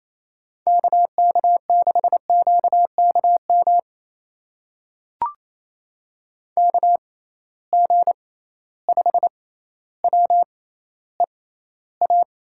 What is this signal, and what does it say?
Morse code, heard as a keyed tone, about 28 wpm, spaced as for 14 wpm, 700 Hz. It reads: KK6QKM KG5WEA
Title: USA Call Signs - 6 Characters - 6x Character Spacing - 28wpm